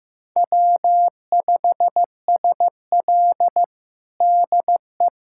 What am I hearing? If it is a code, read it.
Morse code, heard as a keyed tone, about 16 wpm, 700 Hz. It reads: W5SL DE
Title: USA Callsign DE Callsign - 4 Characters - 15wpm